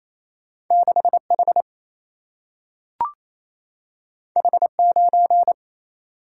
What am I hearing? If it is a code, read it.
Morse code, heard as a keyed tone, about 28 wpm, 700 Hz. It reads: 6H H9
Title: Single-Letter Single-Number 28wpm